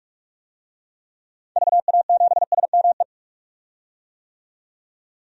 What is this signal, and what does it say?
Morse code, heard as a keyed tone, about 45 wpm, 700 Hz. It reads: VA7SME
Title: DXCC-100 Call Signs - 6 Characters - 45wpm